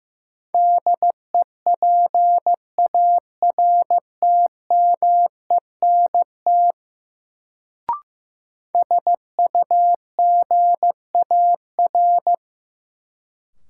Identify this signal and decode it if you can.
Morse code, heard as a keyed tone, about 15 wpm, 700 Hz. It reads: DEPARTMENT SUGAR
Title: Sentences from Top 1200 Words - New Word Drill Alt - 15wpm